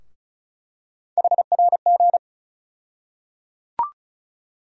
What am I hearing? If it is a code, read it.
Morse code, heard as a keyed tone, about 36 wpm, 700 Hz. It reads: HRG
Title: USA Call Sign Postfixes - 3 Letters 35wpm